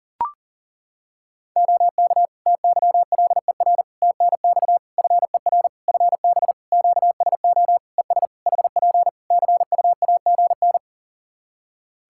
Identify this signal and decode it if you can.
Morse code, heard as a keyed tone, about 40 wpm, 700 Hz. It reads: OK TYLER TNX FER FB QSO ES HP CUAGN
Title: Practice QSO - Exchange 4 - Ending - 40wpm